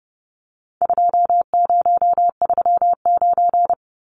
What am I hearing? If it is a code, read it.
Morse code, heard as a keyed tone, about 30 wpm, 700 Hz. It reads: 2039
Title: Numbers - 4-Digit 30wpm